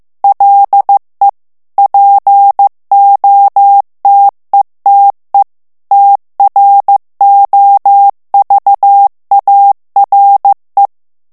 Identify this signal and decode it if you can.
Morse code, heard as a keyed tone, about 15 wpm, 800 Hz. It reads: LE POTETE TROVARE
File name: ITALIEN texte clair 1 vitesse 14 mots minute